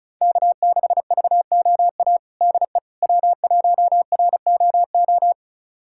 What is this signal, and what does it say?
Morse code, heard as a keyed tone, about 36 wpm, 700 Hz. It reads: K6VOA DE W1ROO
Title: USA Callsign DE Callsign - 5 Characters - 35wpm